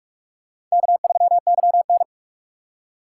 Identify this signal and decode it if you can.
Morse code, heard as a keyed tone, about 46 wpm, 700 Hz. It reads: K3YN
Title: USA Call Signs - 4 Characters - 2x Character Spacing - 45wpm